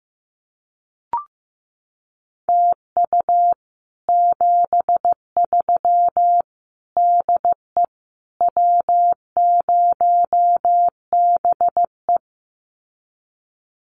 Morse code, heard as a keyed tone, about 15 wpm, 700 Hz. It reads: TU 73 DE W0BE
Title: Common QSO Phrases Encore 15wpm